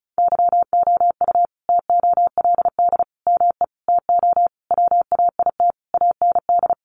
Morse code, heard as a keyed tone, about 34 wpm, 700 Hz. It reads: YOU TOLD ME TO WAIT AND
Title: Sentences from Top 1000 Words 35wpm